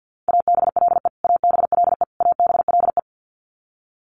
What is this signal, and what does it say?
Morse code, heard as a keyed tone, about 50 wpm, 700 Hz. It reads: ABLE ABLE ABLE
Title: Top 500 Words Repeated 50wpm